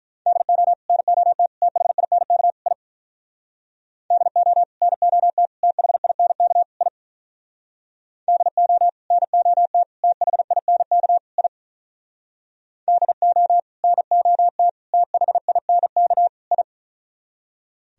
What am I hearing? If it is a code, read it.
Morse code, heard as a keyed tone, about 43 wpm, 700 Hz. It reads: DO NOT THINK I DO NOT THINK I DO NOT THINK I DO NOT THINK I
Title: Speed-Racing - Sets of 4 Words 35wpm